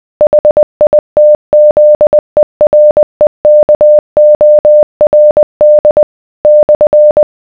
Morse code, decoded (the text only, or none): HITZEREKORD /